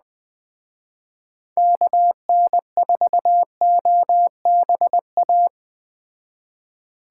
Morse code, heard as a keyed tone, about 20 wpm, 700 Hz. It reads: KN4OBA
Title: Speed Racing - USA 2x3 Call Signs 17wpm